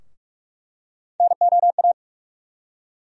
Morse code, heard as a keyed tone, about 45 wpm, 700 Hz. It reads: NOA
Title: USA Call Sign Postfixes - 3 Letters 45wpm